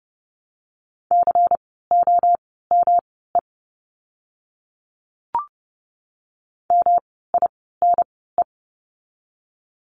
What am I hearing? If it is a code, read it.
Morse code, heard as a keyed tone, about 30 wpm, spaced as for 17 wpm, 700 Hz. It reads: COME MINE
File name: course / 30:10wpm / 108 Lesson 18f - Letter m - Words 30wpm